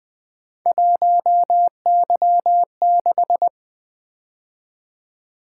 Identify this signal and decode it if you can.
Morse code, heard as a keyed tone, about 24 wpm, 700 Hz. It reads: 1Y6
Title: Speed-Racing - Three Letter-Number Combinations - 20wpm